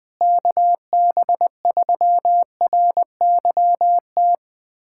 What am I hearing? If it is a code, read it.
Morse code, heard as a keyed tone, about 20 wpm, 700 Hz. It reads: KB3RYT
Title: USA Callsign DE Callsign - 6 Characters - 20wpm